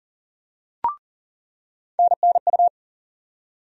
Morse code, heard as a keyed tone, about 40 wpm, 700 Hz. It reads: NNU
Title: USA Call Sign Postfixes - 3 Letters 40wpm